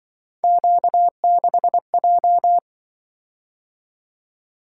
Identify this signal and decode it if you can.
Morse code, heard as a keyed tone, about 24 wpm, 700 Hz. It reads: Q6J
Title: Speed-Racing - Three Letter-Number Combinations - 20wpm